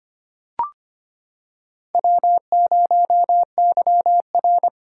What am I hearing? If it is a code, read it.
Morse code, heard as a keyed tone, about 25 wpm, 700 Hz. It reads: W0YR
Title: K1USN SST QSO Practice - 25wpm